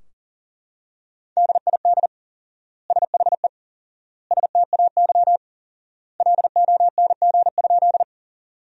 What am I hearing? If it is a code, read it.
Morse code, heard as a keyed tone, about 37 wpm, 700 Hz. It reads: DID SHE STAY LONG?
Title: 4x Word-Spacing - Sentences from Top 500 Words - 40wpm